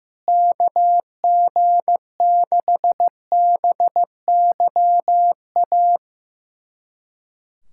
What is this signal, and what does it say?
Morse code, heard as a keyed tone, about 15 wpm, 700 Hz. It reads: KG6BYA
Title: USA Callsign DE Callsign 15wpm